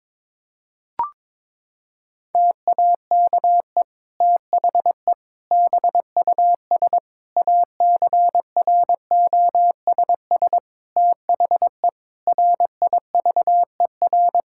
Morse code, heard as a keyed tone, about 22 wpm, 700 Hz. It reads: TAKE THE BUS ACROSS THE RIVER
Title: Sentences from Top 1000 Words - Review Part 6 - 22wpm